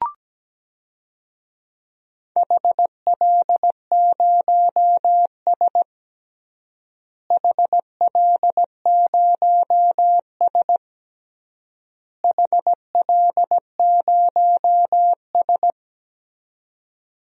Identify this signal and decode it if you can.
Morse code, heard as a keyed tone, about 17 wpm, 700 Hz. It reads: HL0S HL0S HL0S
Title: DXCC-100 Call Signs - 4 Characters - Repeated 3x - 17wpm